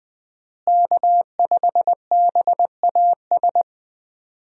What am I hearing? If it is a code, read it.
Morse code, heard as a keyed tone, about 20 wpm, 700 Hz. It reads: K5BAS
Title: USA Call Signs - 5 Characters - Repeated 3x - 20wpm